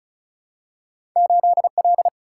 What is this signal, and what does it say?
Morse code, heard as a keyed tone, about 36 wpm, 700 Hz. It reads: 8L
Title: Single-Letter Single-Number 35wpm